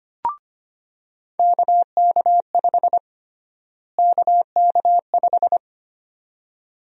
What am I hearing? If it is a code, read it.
Morse code, heard as a keyed tone, about 25 wpm, 700 Hz. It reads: KK5 KK5
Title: USA Call Sign Prefixes - Repeated 2x - 25wpm